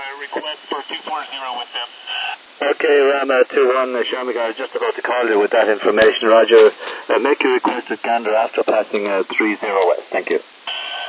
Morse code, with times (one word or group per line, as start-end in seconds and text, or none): none